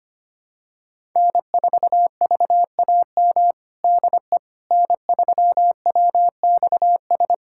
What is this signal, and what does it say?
Morse code, heard as a keyed tone, about 25 wpm, 700 Hz. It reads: N4VAM DE N3WXS
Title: USA Callsign DE Callsign - 5 Characters - 25wpm